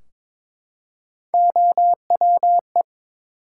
Morse code, heard as a keyed tone, about 22 wpm, 700 Hz. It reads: OWE
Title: Three Letter Words - 22wpm